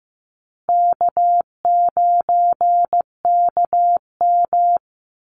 Morse code, heard as a keyed tone, about 15 wpm, 700 Hz. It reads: K9KM
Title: Easy Call Signs Repeated 2x - 15wpm